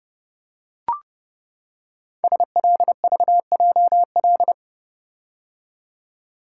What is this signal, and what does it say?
Morse code, heard as a keyed tone, about 28 wpm, 700 Hz. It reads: SLVJL
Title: Code Groups - 5 Letters - 30wpm